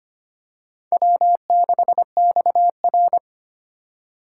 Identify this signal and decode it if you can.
Morse code, heard as a keyed tone, about 25 wpm, 700 Hz. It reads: W6XR
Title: K1USN SST QSO Practice - 25wpm